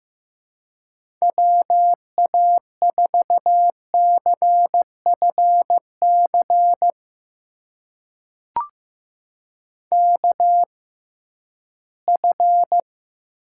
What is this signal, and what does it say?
Morse code, heard as a keyed tone, about 15 wpm, spaced as for 11 wpm, 700 Hz. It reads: WA4CFC KF
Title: USA Call Signs - 6 Characters - 6x Character Spacing - 15wpm